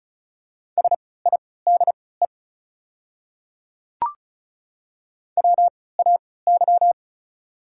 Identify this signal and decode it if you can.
Morse code, heard as a keyed tone, about 35 wpm, spaced as for 20 wpm, 700 Hz. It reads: SIDE WAY
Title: 3x Character Spacing - Top 100 Words - 35wpm